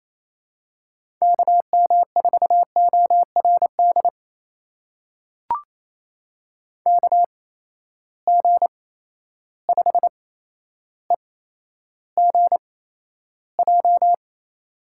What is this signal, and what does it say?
Morse code, heard as a keyed tone, about 28 wpm, spaced as for 12 wpm, 700 Hz. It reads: KM4ORD KG5EGJ
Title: USA Call Signs - 8x Character Spacing - 28wpm